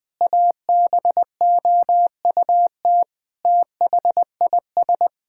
Morse code, heard as a keyed tone, about 20 wpm, 700 Hz. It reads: ABOUT THIS
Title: Sentences with 9-Word Phrases - Spoken Beginning - 20wpm